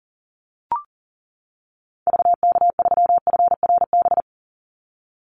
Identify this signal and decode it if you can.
Morse code, heard as a keyed tone, about 42 wpm, 700 Hz. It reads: VK3FRB
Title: USA Call Signs 40wpm